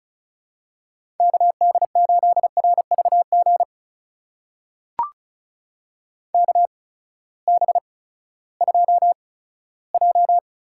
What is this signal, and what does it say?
Morse code, heard as a keyed tone, about 36 wpm, spaced as for 18 wpm, 700 Hz. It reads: KD8RVG KB2J
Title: USA Call Signs - 6 Characters - 8x Character Spacing - 35wpm